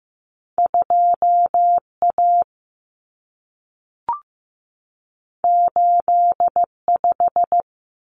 Morse code, heard as a keyed tone, about 15 wpm, 700 Hz. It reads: 2A 85
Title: Common Cut Numbers - 2-Digit 15wpm